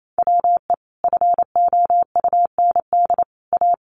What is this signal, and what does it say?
Morse code, heard as a keyed tone, about 28 wpm, 700 Hz. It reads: WE FOUND A
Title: Sentences from Top 1100 Words 28wpm